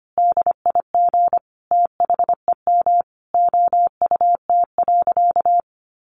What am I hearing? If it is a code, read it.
Morse code, heard as a keyed tone, about 25 wpm, 700 Hz. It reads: DIG THEM OUT.
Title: A Sound Of Thunder 25wpm